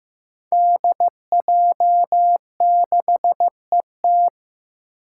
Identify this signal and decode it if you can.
Morse code, heard as a keyed tone, about 15 wpm, 700 Hz. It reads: DJ6ET
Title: Five Challenging Letter-Number Combinations 15wpm